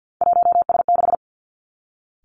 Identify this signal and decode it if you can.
Morse code, heard as a keyed tone, about 50 wpm, 700 Hz. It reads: 1S6
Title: Three Challenging Letter-Number Combinations 50wpm